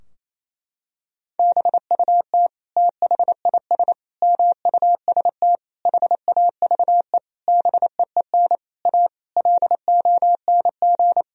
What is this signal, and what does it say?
Morse code, heard as a keyed tone, about 28 wpm, 700 Hz. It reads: BUT THIS MUST HAVE BEEN A LONG
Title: Sentences from Top 1000 Words - Review Part 3 - 28wpm